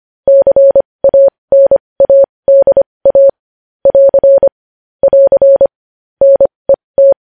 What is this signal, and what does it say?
Morse code, heard as a keyed tone, about 25 wpm, spaced as for 20 wpm, 550 Hz. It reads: CANADA <AR> <AR> NET
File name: podcast / News headlines-1x-25x20